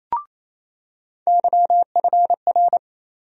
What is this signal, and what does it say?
Morse code, heard as a keyed tone, about 28 wpm, 700 Hz. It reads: YFR
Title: USA Call Sign Postfixes - 3 Letters 28wpm